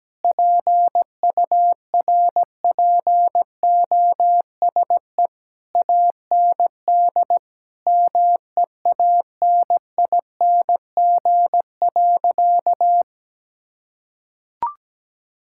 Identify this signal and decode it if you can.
Morse code, heard as a keyed tone, about 17 wpm, 700 Hz. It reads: PURPOSE AND MEANING.
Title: Sentences from Top 1000 Words - Review Part 6 - 17wpm